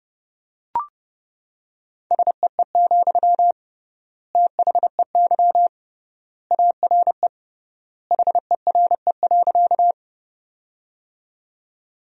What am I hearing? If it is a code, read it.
Morse code, heard as a keyed tone, about 30 wpm, 700 Hz. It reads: SEE, THEY ARE HERE.
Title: 3x Word-Spacing - Sentences from Top 200 Words - 30wpm